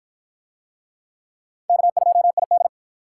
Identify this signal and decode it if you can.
Morse code, heard as a keyed tone, about 47 wpm, 700 Hz. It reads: K2ID